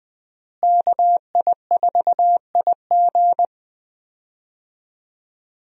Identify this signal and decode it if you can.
Morse code, heard as a keyed tone, about 21 wpm, 700 Hz. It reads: KI4IG
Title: USA Call Signs - 5 Characters - Repeated 3x - 20wpm